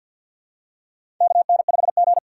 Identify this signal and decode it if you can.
Morse code, heard as a keyed tone, about 50 wpm, 700 Hz. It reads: KN5G
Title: USA Call Signs - 4 Characters - 3x Character Spacing - 50wpm